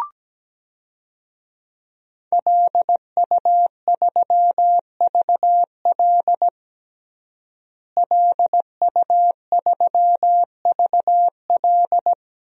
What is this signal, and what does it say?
Morse code, heard as a keyed tone, about 17 wpm, 700 Hz. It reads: LU3VL LU3VL
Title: DXCC-100 Call Signs - 5 Characters - Repeated 2x - 17wpm